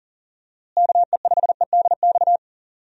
Morse code, heard as a keyed tone, about 37 wpm, 700 Hz. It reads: KE5EDX
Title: USA Call Signs - 6 Characters - Repeated 2x - 40wpm